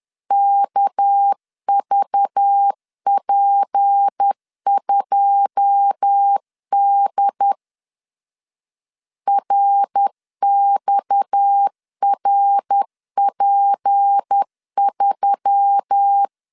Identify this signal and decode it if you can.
Morse code, heard as a keyed tone, about 10 wpm, 800 Hz. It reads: KVP2D RXRP3
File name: LEZIONI / 9 2 10wpm 2736